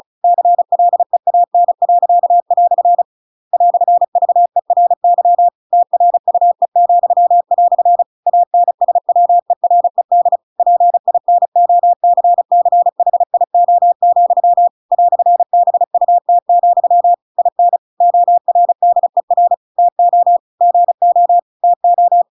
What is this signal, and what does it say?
Morse code, heard as a keyed tone, about 34 wpm, 700 Hz. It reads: CLEAN." "VERY TRUE," ANSWERED PINOCCHIO, "BUT, IN ORDER TO GO TO
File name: Pinocchio / Pinocchio ch8 35 35 ik1hge